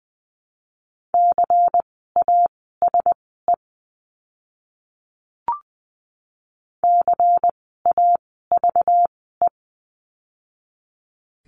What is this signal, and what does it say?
Morse code, heard as a keyed tone, about 20 wpm, spaced as for 14 wpm, 700 Hz. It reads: CASE CAVE
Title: 192 Lesson 31f - Letter v - Words 20wpm